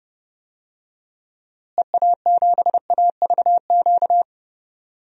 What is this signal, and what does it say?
Morse code, heard as a keyed tone, about 32 wpm, 700 Hz. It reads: EA7AVQ
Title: DXCC-100 Call Signs - 6 Characters - Repeated 3x - 30wpm